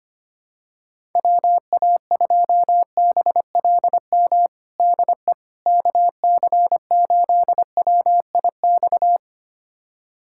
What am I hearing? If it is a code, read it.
Morse code, heard as a keyed tone, about 25 wpm, 700 Hz. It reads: WA2BLM DE KC8WIX
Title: USA Callsign DE Callsign 25wpm